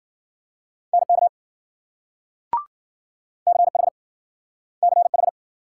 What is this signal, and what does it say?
Morse code, heard as a keyed tone, about 53 wpm, 700 Hz. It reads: NK CH CH